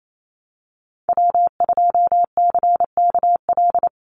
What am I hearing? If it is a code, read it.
Morse code, heard as a keyed tone, about 28 wpm, 700 Hz. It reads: W2CKL
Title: NA SOTA Activator Callsigns 28wpm